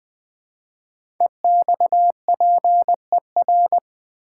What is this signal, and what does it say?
Morse code, heard as a keyed tone, about 20 wpm, 700 Hz. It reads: EXPER
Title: Build A Word - 5 Letters At A Time - 10-Letter Words - 20wpm